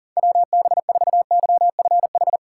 Morse code, heard as a keyed tone, about 40 wpm, 700 Hz. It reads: WB4YFH
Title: USA Call Signs - 6 Characters - 6x Character Spacing - 40wpm